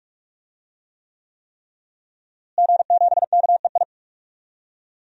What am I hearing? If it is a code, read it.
Morse code, heard as a keyed tone, about 44 wpm, 700 Hz. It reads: G7KEI
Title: DXCC-100 Call Signs - 5 Characters - Repeated 3x - 45wpm